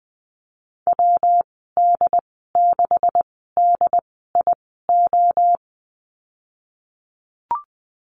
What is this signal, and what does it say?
Morse code, heard as a keyed tone, about 20 wpm, spaced as for 15 wpm, 700 Hz. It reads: WD6DIO
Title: 128 Lesson 21g - Number 6 - Call Signs 20wpm